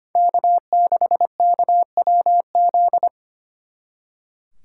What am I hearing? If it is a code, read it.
Morse code, heard as a keyed tone, about 33 wpm, 700 Hz. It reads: K6KWZ